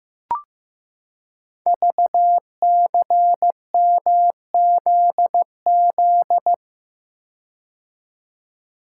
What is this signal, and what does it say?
Morse code, heard as a keyed tone, about 15 wpm, 700 Hz. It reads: VCMZZ
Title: Code Groups - 5 Letters - 15wpm